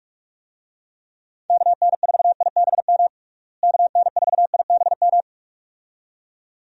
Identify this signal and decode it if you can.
Morse code, heard as a keyed tone, about 46 wpm, 700 Hz. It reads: KN4IBM KN4IBM
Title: USA Call Signs - 6 Characters - Repeated 2x - 45wpm